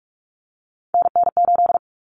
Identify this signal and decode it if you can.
Morse code, heard as a keyed tone, about 44 wpm, 700 Hz. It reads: NN8
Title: Common Cut Numbers - 3-Digit 45wpm